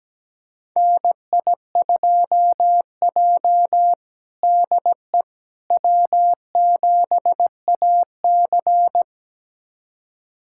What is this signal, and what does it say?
Morse code, heard as a keyed tone, about 17 wpm, 700 Hz. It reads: NI2J DE W7AC
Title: USA Callsign DE Callsign - 4 Characters - 17wpm